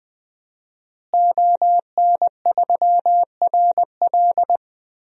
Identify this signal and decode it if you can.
Morse code, heard as a keyed tone, about 21 wpm, 700 Hz. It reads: ON3RL
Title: DXCC-100 Call Signs - 5 Characters - Repeated 3x - 20wpm